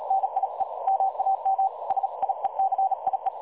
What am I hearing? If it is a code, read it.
Morse code, heard as a keyed tone, about 36 wpm, 770 Hz. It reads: NIE MAM SIECIE
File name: dlazaawansowanych / nagraniazpasma / websdr recording 3565.0kHz